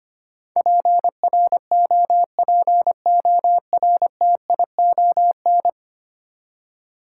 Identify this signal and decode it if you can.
Morse code, heard as a keyed tone, about 25 wpm, 700 Hz. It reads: PROPORTION
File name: 10 Letter Words - First 3 Letters Given - 25wpm